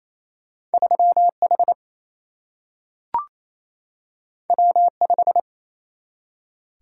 Morse code, heard as a keyed tone, about 28 wpm, 700 Hz. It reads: 3H W5
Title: Single-Letter Single-Number 28wpm